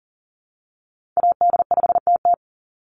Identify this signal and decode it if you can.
Morse code, heard as a keyed tone, about 40 wpm, 700 Hz. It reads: AD5TT
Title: Call Signs - 40wpm